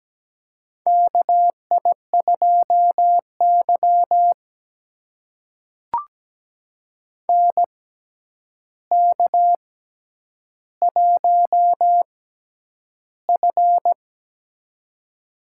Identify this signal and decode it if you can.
Morse code, heard as a keyed tone, about 17 wpm, spaced as for 9 wpm, 700 Hz. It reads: KI2Y NK1F
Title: USA Call Signs - 4 Characters - 6x Character Spacing - 17wpm